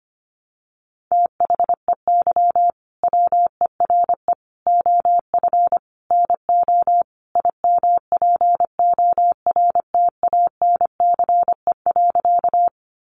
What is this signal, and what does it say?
Morse code, heard as a keyed tone, about 25 wpm, 700 Hz. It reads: THEY WERE OF NO IMPORTANCE.
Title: Sentences from Top 1200 Words 25wpm